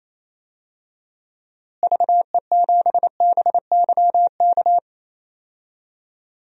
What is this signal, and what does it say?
Morse code, heard as a keyed tone, about 28 wpm, 700 Hz. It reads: VE7BYK
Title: DXCC-100 Call Signs - 6 Characters - Repeated 2x - 28wpm